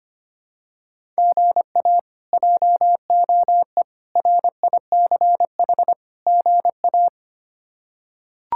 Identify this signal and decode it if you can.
Morse code, heard as a keyed tone, about 25 wpm, 700 Hz. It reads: GA JOE RICH GA
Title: K1USN SST QSO Practice - 25wpm